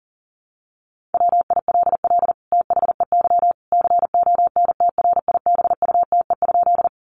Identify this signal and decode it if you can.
Morse code, heard as a keyed tone, about 40 wpm, 700 Hz. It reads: WILL THEY CONTRIBUTE?
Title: Sentences from Top 1400 Words 40wpm